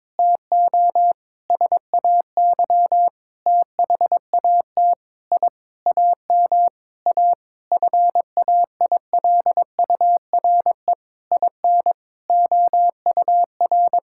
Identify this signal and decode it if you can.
Morse code, heard as a keyed tone, about 22 wpm, 700 Hz. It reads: TO SAY THAT I AM A FAILURE IN OUR
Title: Sentences from Top 1000 Words - Review Part 6 - 22wpm